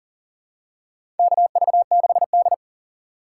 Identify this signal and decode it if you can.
Morse code, heard as a keyed tone, about 40 wpm, 700 Hz. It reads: KV6D
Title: USA Call Signs - 4 Characters - 2x Character Spacing - 40wpm